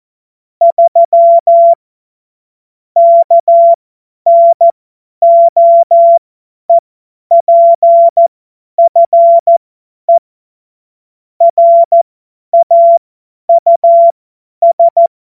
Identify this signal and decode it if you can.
Morse code, heard as a keyed tone, about 14 wpm, spaced as for 10 wpm, 680 Hz. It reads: 3 KNOEPFE RAUS
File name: Morsekurs / Witze / Witz 03. August 2021, 50 F70